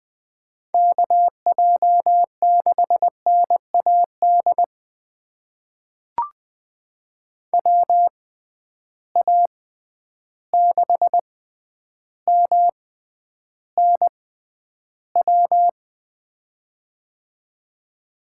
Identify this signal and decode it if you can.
Morse code, heard as a keyed tone, about 20 wpm, spaced as for 10 wpm, 700 Hz. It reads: KJ6NAD WA6MNW
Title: USA Call Signs - 6 Characters - 6x Character Spacing - 20wpm